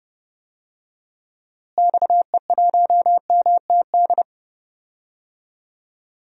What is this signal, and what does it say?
Morse code, heard as a keyed tone, about 30 wpm, 700 Hz. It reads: XE1MTD
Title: DXCC-100 Call Signs - 6 Characters - Repeated 2x - 30wpm